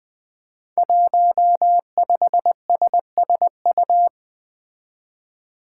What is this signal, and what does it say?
Morse code, heard as a keyed tone, about 20 wpm, 700 Hz. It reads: 15SSU
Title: Five Challenging Letter-Number Combinations 20wpm